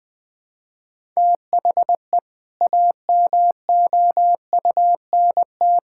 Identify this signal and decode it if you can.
Morse code, heard as a keyed tone, about 20 wpm, 700 Hz. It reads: THE AMOUNT
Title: Build A Phrase - 3-Word Phrases - 20wpm